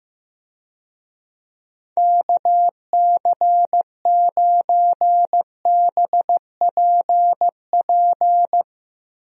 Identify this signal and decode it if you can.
Morse code, heard as a keyed tone, about 15 wpm, 700 Hz. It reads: KC9BPP
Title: USA Call Signs - 6 Characters - 15wpm